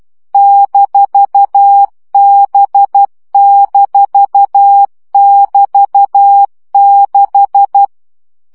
Morse code, heard as a keyed tone, about 12 wpm, 800 Hz. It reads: -B-<BT>6
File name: CW gyakorlas b 6 egyenlo minusz 10 baud